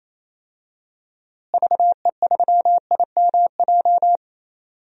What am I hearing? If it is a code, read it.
Morse code, heard as a keyed tone, about 28 wpm, 700 Hz. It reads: VE3IMJ
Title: DXCC-100 Call Signs - 6 Characters - Repeated 2x - 28wpm